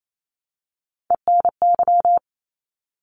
Morse code, heard as a keyed tone, about 28 wpm, 700 Hz. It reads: ENY
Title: ARRL Sections 28wpm